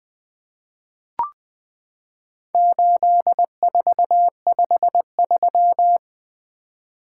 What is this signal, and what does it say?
Morse code, heard as a keyed tone, about 20 wpm, 700 Hz. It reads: 8453
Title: Four Letter-Number Combinations - 20wpm